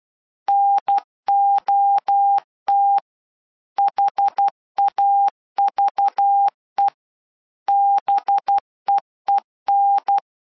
Morse code, heard as a keyed tone, about 12 wpm, 800 Hz. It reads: NOT HAVE BEEN